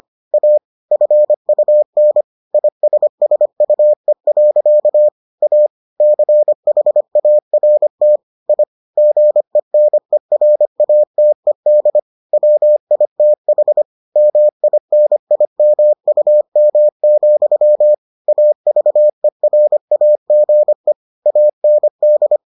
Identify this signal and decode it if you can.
Morse code, heard as a keyed tone, about 25 wpm, 600 Hz. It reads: A FUN ISSUE. A CHART I GENERATED WITH MINIMUM, AVERAGE AND